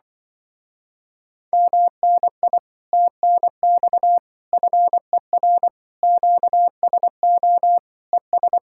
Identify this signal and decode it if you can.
Morse code, heard as a keyed tone, about 25 wpm, 700 Hz. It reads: MNI TNX FER QSO ES